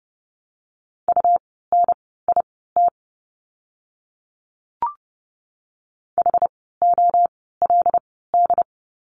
Morse code, heard as a keyed tone, about 30 wpm, spaced as for 17 wpm, 700 Hz. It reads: UNIT HOLD
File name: Lesson 17f - Letter u - Words 30wpm